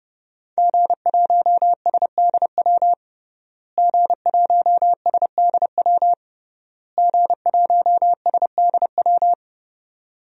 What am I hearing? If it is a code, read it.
Morse code, heard as a keyed tone, about 30 wpm, 700 Hz. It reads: G1SDW G1SDW G1SDW
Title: DXCC-100 Call Signs - 5 Characters - Repeated 3x - 30wpm